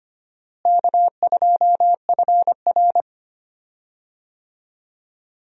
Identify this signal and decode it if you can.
Morse code, heard as a keyed tone, about 25 wpm, 700 Hz. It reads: K2FR
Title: USA Call Signs - 4 Characters - Repeated 2x - 25wpm